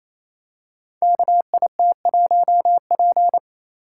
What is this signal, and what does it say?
Morse code, heard as a keyed tone, about 29 wpm, 700 Hz. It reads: KIT1P
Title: Code Groups - 5 Letter:Numbers - 28wpm